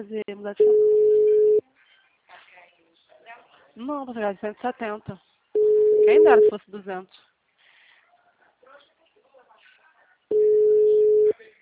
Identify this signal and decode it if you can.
Ringback tone (Europe, 425 Hz), with speech.